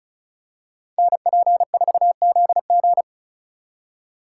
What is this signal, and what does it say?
Morse code, heard as a keyed tone, about 33 wpm, 700 Hz. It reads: NP4ZG
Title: Speed Racing - USA 2x2 Call Signs 35wpm